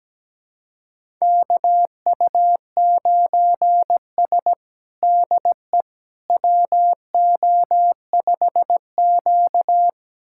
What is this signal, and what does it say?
Morse code, heard as a keyed tone, about 17 wpm, 700 Hz. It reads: KU9S DE WO5Q
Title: USA Callsign DE Callsign - 4 Characters - 17wpm